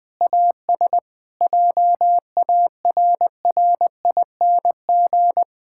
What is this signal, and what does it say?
Morse code, heard as a keyed tone, about 20 wpm, 700 Hz. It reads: AS JARRING
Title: Pangram Challenge v2 - 20wpm